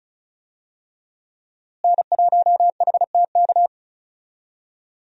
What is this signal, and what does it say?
Morse code, heard as a keyed tone, about 36 wpm, 700 Hz. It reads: N1HTK
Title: USA Call Signs - 5 Characters - Repeated 3x - 35wpm